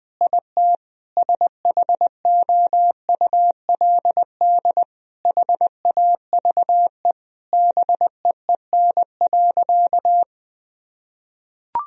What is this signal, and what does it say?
Morse code, heard as a keyed tone, about 20 wpm, 700 Hz. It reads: IT SHOULD HAVE BEEN.
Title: Sentences from Top 1000 Words - Review Part 4 - 20wpm